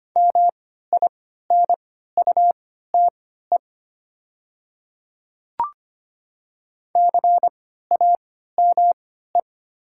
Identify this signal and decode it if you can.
Morse code, heard as a keyed tone, about 25 wpm, spaced as for 14 wpm, 690 Hz. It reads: MINUTE CAME
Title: 3x Character Spacing - Top 500 Words - 25wpm